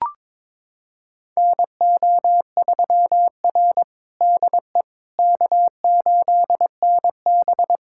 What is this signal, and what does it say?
Morse code, heard as a keyed tone, about 22 wpm, 700 Hz. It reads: NO3R DE K8NB
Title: USA Callsign DE Callsign - 4 Characters - 22wpm